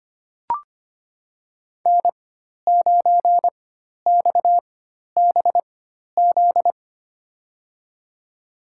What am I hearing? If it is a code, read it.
Morse code, heard as a keyed tone, about 25 wpm, spaced as for 12 wpm, 700 Hz. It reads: N9XBZ
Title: USA Call Signs - 4x Character Spacing - 25wpm